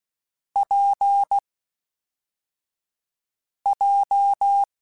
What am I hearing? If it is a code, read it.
Morse code, heard as a keyed tone, about 16 wpm, 800 Hz. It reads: P J
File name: Difficult Combos / Difficult Combos-mp3s / WPJ WPJ1 @10